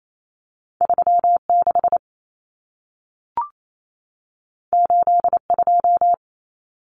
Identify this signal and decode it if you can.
Morse code, heard as a keyed tone, about 28 wpm, 700 Hz. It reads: 36 82
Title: Common Cut Numbers - 2-Digit 28wpm